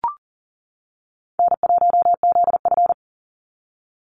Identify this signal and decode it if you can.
Morse code, heard as a keyed tone, about 40 wpm, 700 Hz. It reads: N1ZF
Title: NA SOTA Chaser Callsigns 40wpm